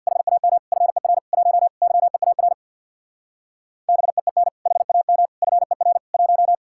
Morse code, heard as a keyed tone, about 64 wpm, 700 Hz. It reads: HAM FER 1 YEAR BEEN HAM FER 1